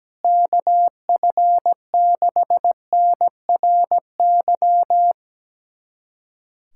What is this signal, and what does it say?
Morse code, heard as a keyed tone, about 17 wpm, 700 Hz. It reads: KF6NRY
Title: USA Callsign DE Callsign 17wpm